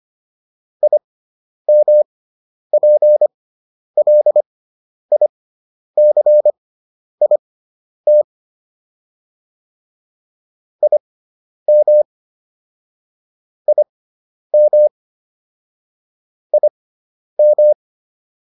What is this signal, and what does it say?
Morse code, heard as a keyed tone, about 25 wpm, spaced as for 10 wpm, 600 Hz. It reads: IMPLICIT IM IM IM